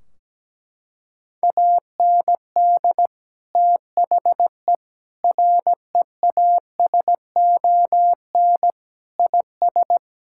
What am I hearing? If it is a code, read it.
Morse code, heard as a keyed tone, about 17 wpm, 700 Hz. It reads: AND THE REASON IS
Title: Sentences from Top 1600 Words 17wpm